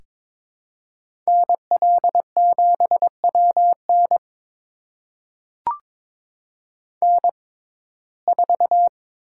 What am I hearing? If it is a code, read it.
Morse code, heard as a keyed tone, about 22 wpm, spaced as for 16 wpm, 700 Hz. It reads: NL7WN N4
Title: USA Call Signs - 5 Characters - 6x Character Spacing - 22wpm